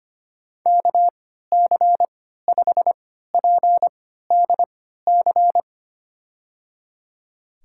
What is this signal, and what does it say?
Morse code, heard as a keyed tone, about 25 wpm, spaced as for 14 wpm, 700 Hz. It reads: KC5PDC
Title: USA Call Signs - 6 Characters - 3x Character Spacing - 25wpm